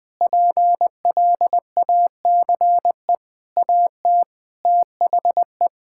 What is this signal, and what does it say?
Morse code, heard as a keyed tone, about 20 wpm, 700 Hz. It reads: PLACE AT THE
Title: Sets of 8 Words - 20wpm